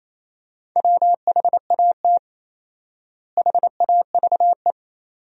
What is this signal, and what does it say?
Morse code, heard as a keyed tone, about 27 wpm, 700 Hz. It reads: WHAT HAVE
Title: 4x Word-Spacing - Sentences from Top 500 Words - 28wpm